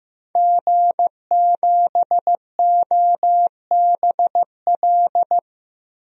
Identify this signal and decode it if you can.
Morse code, heard as a keyed tone, about 15 wpm, 700 Hz. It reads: G7OBL
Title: DXCC-100 Call Signs - 5 Characters - Repeated 3x - 15wpm